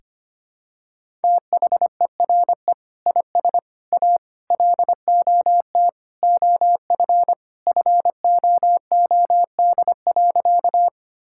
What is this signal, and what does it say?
Morse code, heard as a keyed tone, about 25 wpm, 700 Hz. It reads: THERE IS A LOT OF FOOD.
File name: practice / 25wpm / Sentences with 3-Word Phrases - 25wpm